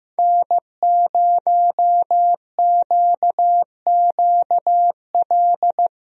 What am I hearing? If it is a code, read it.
Morse code, heard as a keyed tone, about 15 wpm, 700 Hz. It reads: N0QQL
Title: USA Call Signs - 5 Characters - Repeated 3x - 15wpm